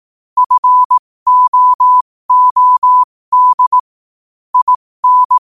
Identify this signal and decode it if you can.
Morse code, heard as a keyed tone, about 18 wpm, spaced as for 15 wpm, 1000 Hz. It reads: FOOD IN
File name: qotdfiles / 15wpm